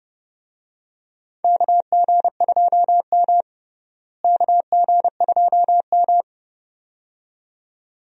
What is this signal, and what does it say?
Morse code, heard as a keyed tone, about 30 wpm, 700 Hz. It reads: KG2M KG2M
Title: USA Call Signs - 4 Characters - Repeated 2x - 30wpm